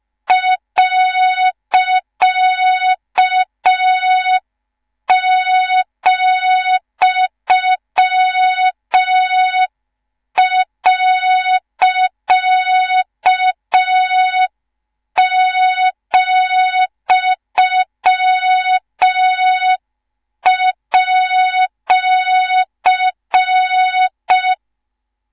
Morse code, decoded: .,.,@